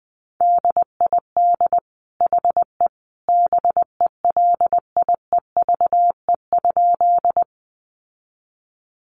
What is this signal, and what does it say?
Morse code, heard as a keyed tone, about 20 wpm, 700 Hz. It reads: DID HE BELIEVE?
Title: Sentences from Top 1000 Words 20wpm